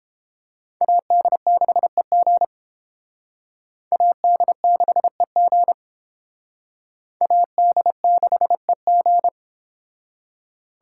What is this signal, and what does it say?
Morse code, heard as a keyed tone, about 28 wpm, 700 Hz. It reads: AD6EG AD6EG AD6EG